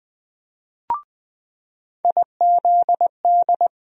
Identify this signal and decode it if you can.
Morse code, heard as a keyed tone, about 20 wpm, 700 Hz. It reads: IZD
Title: USA Call Sign Postfixes - 3 Letters 20wpm